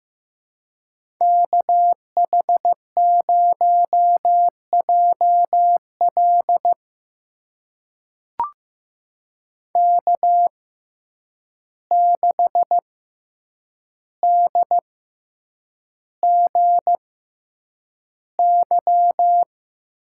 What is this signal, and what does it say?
Morse code, heard as a keyed tone, about 15 wpm, spaced as for 8 wpm, 700 Hz. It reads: KH0JL K6DGY
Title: USA Call Signs - 5 Characters - 6x Character Spacing - 15wpm